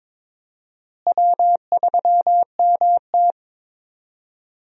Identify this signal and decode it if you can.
Morse code, heard as a keyed tone, about 22 wpm, 700 Hz. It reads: W3MT
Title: USA Call Signs - 4 Characters - Repeated 3x - 22wpm